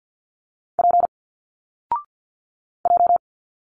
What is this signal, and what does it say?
Morse code, heard as a keyed tone, about 50 wpm, 700 Hz. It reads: P J
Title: Single Letters 50wpm